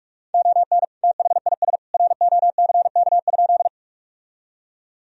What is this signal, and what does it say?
Morse code, heard as a keyed tone, about 46 wpm, 700 Hz. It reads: ON THIS ROCK?
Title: Sentences from Top 1000 Words - Review Part 2 - 45wpm